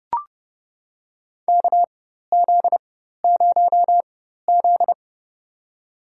Morse code, heard as a keyed tone, about 31 wpm, spaced as for 14 wpm, 700 Hz. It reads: KZ0Z
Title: USA Call Signs - 4 Characters - 4x Character Spacing - 30wpm